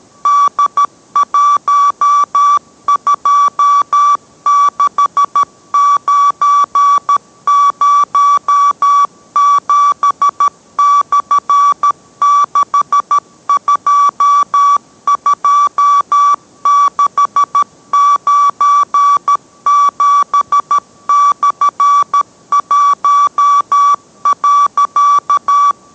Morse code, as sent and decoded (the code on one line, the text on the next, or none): -.. .---- ..--- -.... ----. ----- --... -..-. -.... ..--- ..--- -.... ----. --... -..-. .---- .-.-.-
D126907/622697/1.